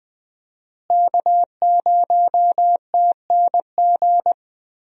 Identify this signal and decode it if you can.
Morse code, heard as a keyed tone, about 20 wpm, 700 Hz. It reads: K0TNG
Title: USA Call Signs - 5 Characters - Repeated 2x - 20wpm